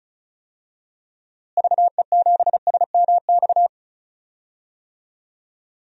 Morse code, heard as a keyed tone, about 34 wpm, 700 Hz. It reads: VE7SMX
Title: DXCC-100 Call Signs - 6 Characters - Repeated 2x - 35wpm